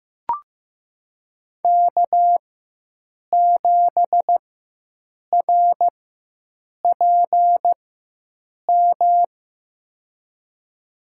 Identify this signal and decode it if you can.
Morse code, heard as a keyed tone, about 15 wpm, spaced as for 7 wpm, 700 Hz. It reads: K7RPM
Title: USA Call Signs - 5 Characters - 4x Character Spacing - 15wpm